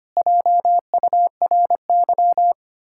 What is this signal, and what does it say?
Morse code, heard as a keyed tone, about 25 wpm, 700 Hz. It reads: JURY
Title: Pangram Challenge v2 - 25wpm